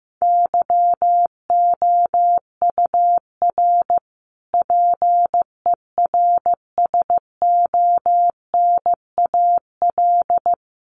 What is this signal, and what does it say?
Morse code, heard as a keyed tone, about 15 wpm, 700 Hz. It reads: YOUR PERSONAL
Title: A Sound Of Thunder 15wpm